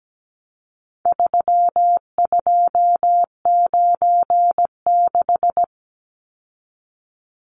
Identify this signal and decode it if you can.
Morse code, heard as a keyed tone, about 17 wpm, 700 Hz. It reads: 3296
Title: Numbers - 4-Digit 17wpm